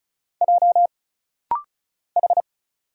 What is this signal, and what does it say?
Morse code, heard as a keyed tone, about 33 wpm, 700 Hz. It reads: J H
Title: Single Letter-Number - Rapid-Fire 35wpm